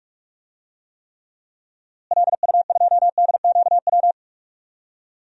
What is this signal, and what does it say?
Morse code, heard as a keyed tone, about 46 wpm, 700 Hz. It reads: LU2DQW
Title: DXCC-100 Call Signs - 6 Characters - Repeated 3x - 45wpm